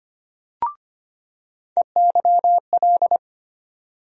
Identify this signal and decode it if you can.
Morse code, heard as a keyed tone, about 25 wpm, 700 Hz. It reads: EYL
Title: USA Call Sign Postfixes - 3 Letters 25wpm